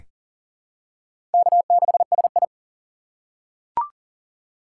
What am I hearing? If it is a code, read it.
Morse code, heard as a keyed tone, about 37 wpm, 700 Hz. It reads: K6SI
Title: USA Call Signs - 4 Characters - 6x Character Spacing - 40wpm